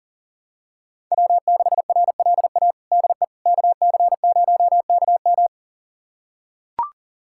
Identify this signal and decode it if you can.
Morse code, heard as a keyed tone, about 40 wpm, 700 Hz. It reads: W6RLA DE KC0KM
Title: USA Callsign DE Callsign - 5 Characters - 40wpm